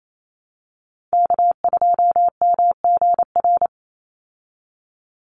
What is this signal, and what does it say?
Morse code, heard as a keyed tone, about 28 wpm, 700 Hz. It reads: K2MGR
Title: Call Signs Repeated 3x - 28wpm